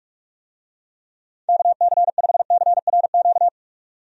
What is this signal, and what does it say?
Morse code, heard as a keyed tone, about 46 wpm, 700 Hz. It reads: KC5CRQ
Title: USA Call Signs - 6 Characters - 45wpm